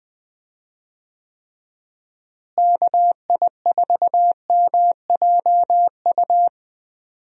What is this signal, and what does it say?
Morse code, heard as a keyed tone, about 20 wpm, 700 Hz. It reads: KI4MJU
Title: USA Call Signs - 6 Characters - Repeated 3x - 20wpm